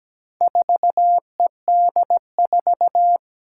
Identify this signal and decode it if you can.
Morse code, heard as a keyed tone, about 17 wpm, 700 Hz. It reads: 4ED4
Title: Four Challenging Letter-Number Combinations 17wpm